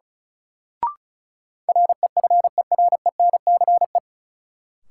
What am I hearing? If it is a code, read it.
Morse code, heard as a keyed tone, about 36 wpm, 700 Hz. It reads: REFERENCE
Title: Sentences from Top 1200 Words - New Word Drill Alt - 35wpm